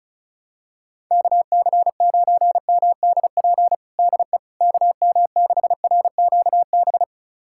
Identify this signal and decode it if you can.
Morse code, heard as a keyed tone, about 36 wpm, 690 Hz. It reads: KC9MDP DE KM6RQB
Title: USA Callsign DE Callsign - 6 Characters - 35wpm